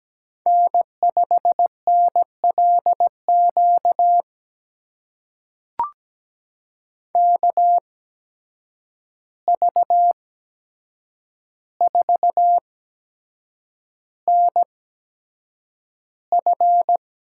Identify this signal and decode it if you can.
Morse code, heard as a keyed tone, about 17 wpm, spaced as for 7 wpm, 700 Hz. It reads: N5NLQ KV4NF
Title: USA Call Signs - 5 Characters - 8x Character Spacing - 17wpm